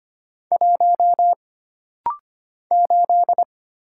Morse code, heard as a keyed tone, about 25 wpm, 700 Hz. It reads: 1 8
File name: Single Character (Letter:Number:Punctuation:Prosign) - Rapid-Fire - 25wpm